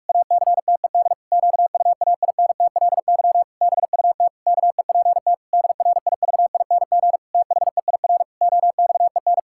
Morse code, heard as a keyed tone, about 41 wpm, 700 Hz. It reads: ACTED QUAINTLY BUT KEPT DRIVING THEIR OXEN
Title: Pangram Challenge v2 - 45wpm